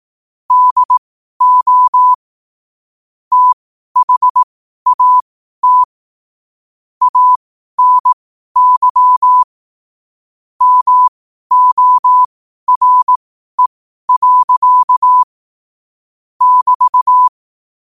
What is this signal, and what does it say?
Morse code, heard as a keyed tone, about 18 wpm, spaced as for 12 wpm, 1000 Hz. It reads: DO THAT ANY MORE. <BT>
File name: qotdfiles / 12wpm